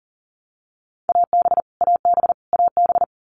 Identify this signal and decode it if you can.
Morse code, heard as a keyed tone, about 40 wpm, 700 Hz. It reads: AB AB AB
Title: Canadian Provinces and Territories - Repeated 3x - 40wpm